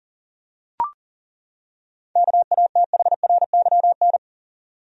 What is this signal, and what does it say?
Morse code, heard as a keyed tone, about 40 wpm, 700 Hz. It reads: KATHRYN
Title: USA Common Names - 40wpm